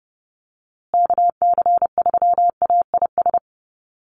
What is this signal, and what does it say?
Morse code, heard as a keyed tone, about 30 wpm, 700 Hz. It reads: KC3AIS
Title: USA Call Signs Repeated 3x 30wpm